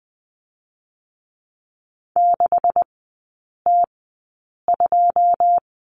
Morse code, heard as a keyed tone, about 20 wpm, spaced as for 8 wpm, 700 Hz. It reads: 6T2
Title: 133 Lesson 22e - Question Mark - Three Characters 20wpm